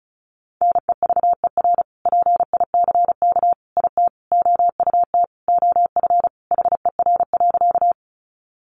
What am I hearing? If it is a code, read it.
Morse code, heard as a keyed tone, about 34 wpm, 700 Hz. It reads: NEVER PICK IT OUT OF HER.
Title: Sentences from Top 1000 Words 35wpm